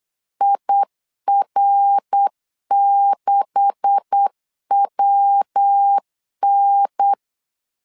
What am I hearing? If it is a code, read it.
Morse code, heard as a keyed tone, about 8 wpm, 800 Hz. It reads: IR6WN
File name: LEZIONI / 10 SIGNS / 10B INT 8wpm 3534